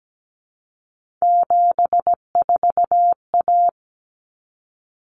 Morse code, heard as a keyed tone, about 17 wpm, 700 Hz. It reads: 74A
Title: Common Cut Numbers - 3-Digit 17wpm